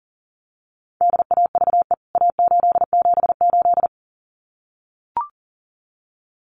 Morse code, heard as a keyed tone, about 40 wpm, 700 Hz. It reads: DAVE A878
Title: CWT Exchange 40wpm